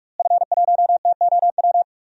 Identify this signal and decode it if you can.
Morse code, heard as a keyed tone, about 43 wpm, 700 Hz. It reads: F1TOW
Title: DXCC-100 Call Signs - 5 Characters - Repeated 2x - 45wpm